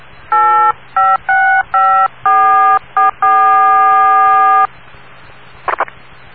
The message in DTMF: 8565444